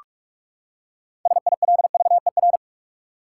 Morse code, heard as a keyed tone, about 42 wpm, 700 Hz. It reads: SILVER
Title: Six Letter Words - 45wpm